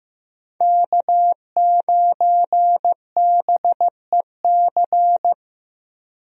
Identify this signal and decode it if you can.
Morse code, heard as a keyed tone, about 15 wpm, 700 Hz. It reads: K9BEC
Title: USA Call Signs - 5 Characters - Repeated 3x - 15wpm